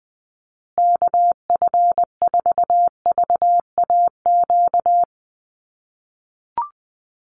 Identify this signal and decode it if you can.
Morse code, heard as a keyed tone, about 20 wpm, 700 Hz. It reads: KF4VAQ
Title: USA Call Signs Repeated 3x 20wpm